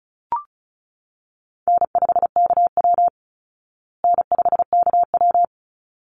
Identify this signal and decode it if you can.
Morse code, heard as a keyed tone, about 34 wpm, 700 Hz. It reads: N5KW N5KW
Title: Easy Call Signs Repeated 2x - 35wpm